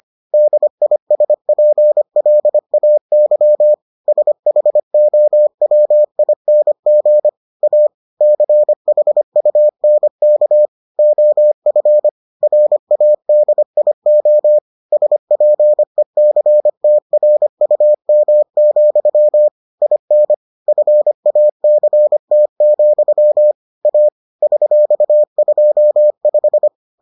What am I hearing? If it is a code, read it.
Morse code, heard as a keyed tone, about 25 wpm, 600 Hz. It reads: DISPLAY SHOWING A CHUNK OF RADIO SPECTRUM, IN FACT, A $25